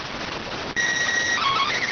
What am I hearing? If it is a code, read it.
SSTV (slow-scan television), PD 290.